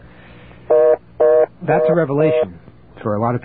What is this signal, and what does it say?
Reorder tone (North America), 480 + 620 Hz, with speech.